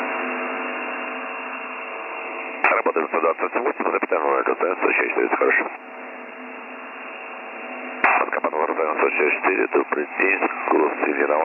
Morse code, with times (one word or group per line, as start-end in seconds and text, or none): none